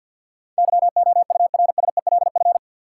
Morse code, heard as a keyed tone, about 52 wpm, 700 Hz. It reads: YOURSELF
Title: Sentences from Top 600 Words - New Word Drill - 50wpm